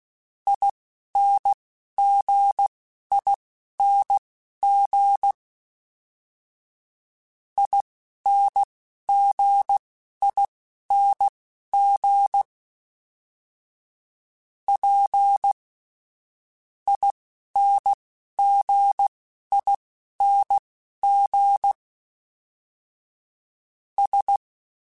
Morse code, heard as a keyed tone, about 16 wpm, spaced as for 11 wpm, 800 Hz. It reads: INGING INGING P INGING S
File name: Special INGs / Special INGs-mp3s / Dble Ings sp @10